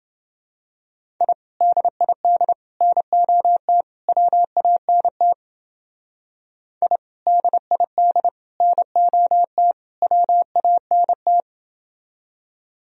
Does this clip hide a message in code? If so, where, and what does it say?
Yes: Morse code from 1.2 to 11.4 s, heard as a keyed tone, about 29 wpm, 700 Hz. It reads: I DID NOT WANT I DID NOT WANT